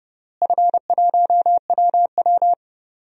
Morse code, heard as a keyed tone, about 30 wpm, 700 Hz. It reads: F1WW
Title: DXCC-100 Call Signs - 4 Characters - Repeated 3x - 30wpm